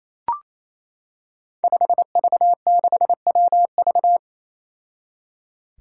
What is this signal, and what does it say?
Morse code, heard as a keyed tone, about 28 wpm, 700 Hz. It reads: 5V6WV
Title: Five Challenging Letter-Number Combinations 28wpm